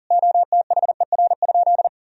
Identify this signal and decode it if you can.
Morse code, heard as a keyed tone, about 40 wpm, 700 Hz. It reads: OTHER?
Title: Sentences from Top 200 Words and Abbreviations 40wpm